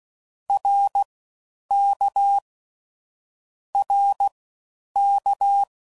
Morse code, heard as a keyed tone, about 16 wpm, spaced as for 9 wpm, 800 Hz. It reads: RK RK